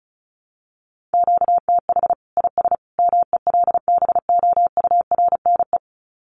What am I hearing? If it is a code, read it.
Morse code, heard as a keyed tone, about 34 wpm, 700 Hz. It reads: QTH IS MELBOURNE
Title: Common QSO Phrases 35wpm